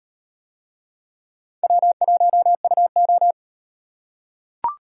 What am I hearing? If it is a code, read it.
Morse code, heard as a keyed tone, about 38 wpm, 700 Hz. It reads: W1UO